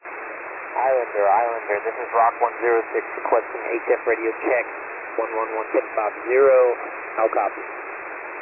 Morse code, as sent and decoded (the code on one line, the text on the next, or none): none